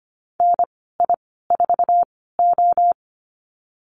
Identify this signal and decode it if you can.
Morse code, heard as a keyed tone, about 25 wpm, spaced as for 16 wpm, 700 Hz. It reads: NI4O
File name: Lesson 15g - Number 5 - Call Signs 25wpm